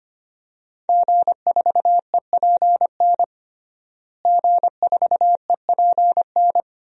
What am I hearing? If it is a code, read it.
Morse code, heard as a keyed tone, about 25 wpm, 700 Hz. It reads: G4EPN G4EPN
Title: DXCC-100 Call Signs - 5 Characters - Repeated 2x - 25wpm